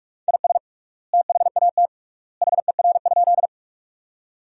Morse code, heard as a keyed tone, about 46 wpm, 690 Hz. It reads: IS THAT HER?
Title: 3x Word-Spacing - Sentences from Top 100 Words - 45wpm